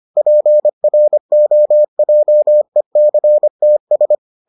Morse code, heard as a keyed tone, about 25 wpm, 600 Hz. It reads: PROJECTS